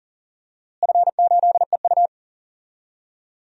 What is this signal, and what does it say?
Morse code, heard as a keyed tone, about 41 wpm, 700 Hz. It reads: F8EU
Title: DXCC-100 Call Signs - 4 Characters - Repeated 2x - 40wpm